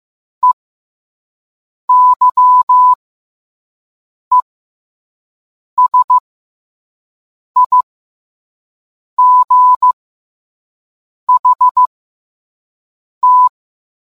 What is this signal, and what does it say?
Morse code, heard as a keyed tone, about 15 wpm, spaced as for 5 wpm, 1000 Hz. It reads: EYESIGHT